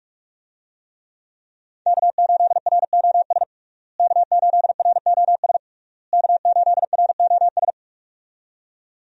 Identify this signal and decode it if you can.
Morse code, heard as a keyed tone, about 46 wpm, 700 Hz. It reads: K8ROS K8ROS K8ROS
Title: USA Call Signs - 5 Characters - Repeated 3x - 45wpm